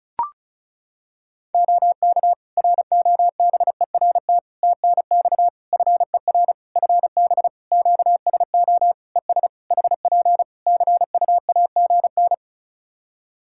Morse code, heard as a keyed tone, about 36 wpm, 700 Hz. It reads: OK ROBERT TNX FER FB QSO ES HP CUAGN
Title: K1USN SST QSO Practice 35wpm